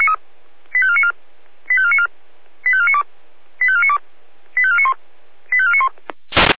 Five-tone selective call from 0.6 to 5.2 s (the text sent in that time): 85494 85494 85492 85492 85491